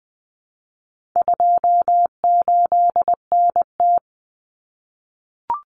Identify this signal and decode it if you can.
Morse code, heard as a keyed tone, about 20 wpm, 700 Hz. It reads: 28NT
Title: Common Cut Numbers - 4-Digit 20wpm